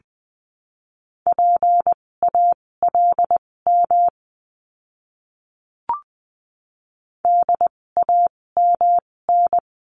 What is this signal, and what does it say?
Morse code, heard as a keyed tone, about 20 wpm, spaced as for 16 wpm, 700 Hz. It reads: PALM DAMN
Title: 256 Lesson 42a - Words 12-20wpm